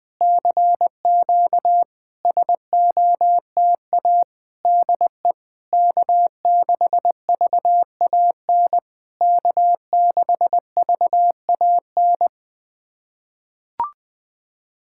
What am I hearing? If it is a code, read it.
Morse code, heard as a keyed tone, about 20 wpm, 690 Hz. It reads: CQ SOTA DE K6VAN K6VAN
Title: SOTA QSO Practice 20wpm